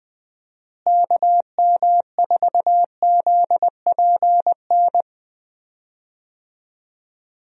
Morse code, heard as a keyed tone, about 20 wpm, 700 Hz. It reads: KM4ZPN
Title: USA Call Signs - 6 Characters - Repeated 2x - 20wpm